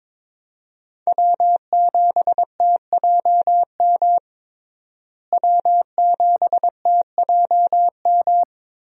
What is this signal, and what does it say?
Morse code, heard as a keyed tone, about 22 wpm, 700 Hz. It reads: W7TJM W7TJM
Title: USA Call Signs - 5 Characters - Repeated 3x - 22wpm